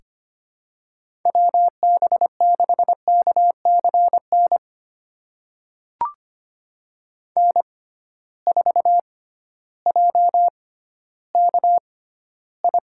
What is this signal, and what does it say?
Morse code, heard as a keyed tone, about 25 wpm, spaced as for 13 wpm, 700 Hz. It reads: WB6KCN N4JKI
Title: USA Call Signs - 6x Character Spacing - 25wpm